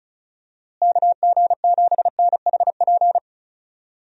Morse code, heard as a keyed tone, about 34 wpm, 700 Hz. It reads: KG7NHP
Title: USA Call Signs - 6 Characters - Repeated 3x - 35wpm